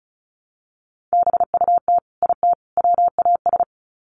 Morse code, heard as a keyed tone, about 34 wpm, 700 Hz. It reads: BUT IT WAS
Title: Sets of 3 Words 35wpm